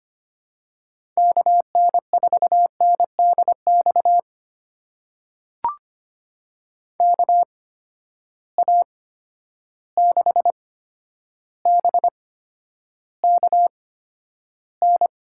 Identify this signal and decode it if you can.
Morse code, heard as a keyed tone, about 25 wpm, spaced as for 11 wpm, 700 Hz. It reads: KN4NDX KA6BKN
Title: USA Call Signs - 8x Character Spacing - 25wpm